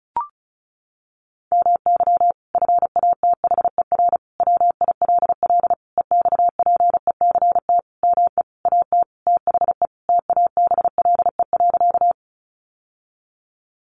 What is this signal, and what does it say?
Morse code, heard as a keyed tone, about 34 wpm, 700 Hz. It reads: MY FATHER WILL EXPECT ME AT THE TABLE.
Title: Sentences from Top 1100 Words 35wpm